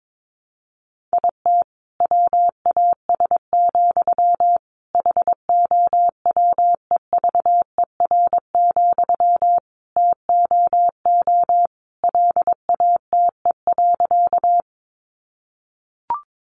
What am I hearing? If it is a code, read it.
Morse code, heard as a keyed tone, about 22 wpm, 700 Hz. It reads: IT WAS, HOWEVER, TOO LATE.
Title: Sentences from Top 600 Words 22wpm